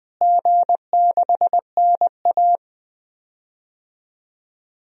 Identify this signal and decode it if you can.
Morse code, heard as a keyed tone, about 20 wpm, 700 Hz. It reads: G6NA
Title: DXCC-100 Call Signs - 4 Characters - Repeated 2x - 20wpm